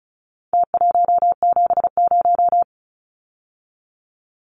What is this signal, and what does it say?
Morse code, heard as a keyed tone, about 34 wpm, 700 Hz. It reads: T170
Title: Numbers - Contest Style - 4-Digit 35wpm